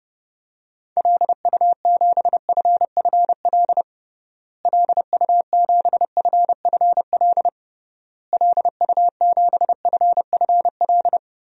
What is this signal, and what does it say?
Morse code, heard as a keyed tone, about 30 wpm, 700 Hz. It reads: LU7FFL LU7FFL LU7FFL
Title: DXCC-100 Call Signs - 6 Characters - Repeated 3x - 30wpm